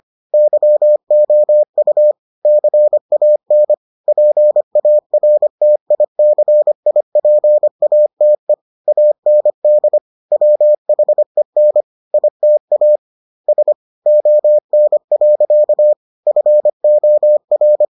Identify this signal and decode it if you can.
Morse code, heard as a keyed tone, about 25 wpm, 600 Hz. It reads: YOU CAN PARTICIPATE AND WHEN ITA S ON. FOR